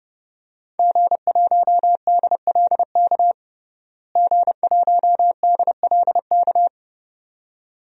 Morse code, heard as a keyed tone, about 30 wpm, 700 Hz. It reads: G1DLK G1DLK
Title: DXCC-100 Call Signs - 5 Characters - Repeated 2x - 30wpm